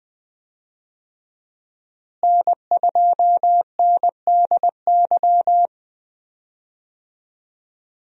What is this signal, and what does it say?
Morse code, heard as a keyed tone, about 20 wpm, 700 Hz. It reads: N2NDY
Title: USA Call Signs - 5 Characters - Repeated 2x - 20wpm